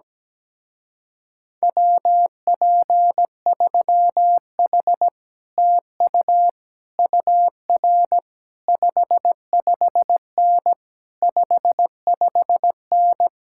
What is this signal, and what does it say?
Morse code, heard as a keyed tone, about 17 wpm, 700 Hz. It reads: WP3H TU UR 55N 55N
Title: Speed-Racing Common QSO Phrases Encore - 17wpm